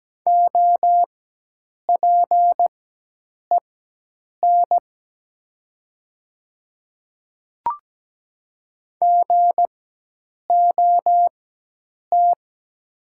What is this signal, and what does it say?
Morse code, heard as a keyed tone, about 17 wpm, spaced as for 8 wpm, 700 Hz. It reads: OPEN GOT
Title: 4x Character Spacing - Top 300 Words - 17wpm